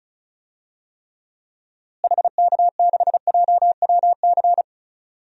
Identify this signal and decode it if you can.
Morse code, heard as a keyed tone, about 36 wpm, 700 Hz. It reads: HK6JWC
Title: DXCC-100 Call Signs - 6 Characters - Repeated 3x - 35wpm